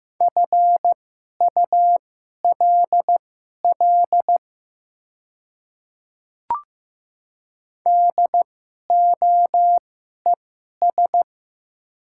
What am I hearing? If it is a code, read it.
Morse code, heard as a keyed tone, about 15 wpm, spaced as for 11 wpm, 700 Hz. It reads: FULL DOES
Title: 2x Character Spacing - Top 500 Words - 15wpm